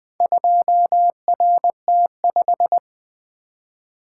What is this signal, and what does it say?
Morse code, heard as a keyed tone, about 20 wpm, 700 Hz. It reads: 2RT5
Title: Four Letter-Number Combinations - 20wpm